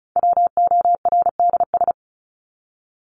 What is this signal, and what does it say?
Morse code, heard as a keyed tone, about 34 wpm, 700 Hz. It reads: WORDS
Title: ES:S Suffix - Top 100 Words with Base Words 35wpm